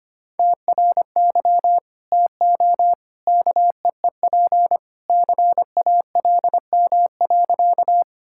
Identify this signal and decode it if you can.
Morse code, heard as a keyed tone, about 25 wpm, 700 Hz. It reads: TRY TO KEEP CALM.
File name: practice / 25wpm / Sentences with 2-Word Phrases - 25wpm